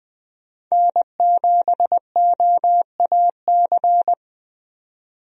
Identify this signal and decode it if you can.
Morse code, heard as a keyed tone, about 20 wpm, 700 Hz. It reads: N7OAC
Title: USA Call Signs - 5 Characters - 20wpm